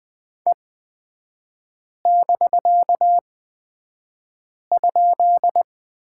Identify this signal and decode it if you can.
Morse code, heard as a keyed tone, about 20 wpm, spaced as for 5 wpm, 700 Hz. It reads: E<BK>?
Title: 250 Lesson 40e - BK - Three Characters 20wpm